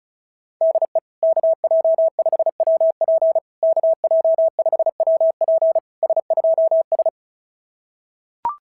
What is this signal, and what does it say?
Morse code, heard as a keyed tone, about 36 wpm, 660 Hz. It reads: DE KJ5WP KJ5WP S2S
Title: SOTA QSO Practice 35wpm